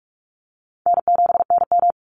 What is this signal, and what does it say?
Morse code, heard as a keyed tone, about 44 wpm, 700 Hz. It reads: N7NM
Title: Easy Call Signs Repeated 2x - 45wpm